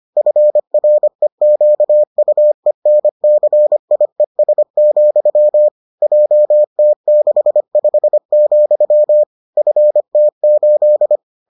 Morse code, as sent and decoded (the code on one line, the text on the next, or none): ..-. .-. --.- ..- . -. -.-. .. ... --..-- / .--- - -.... ..... --..-- / ..-. - ---..
FREQUENCIES, JT65, FT8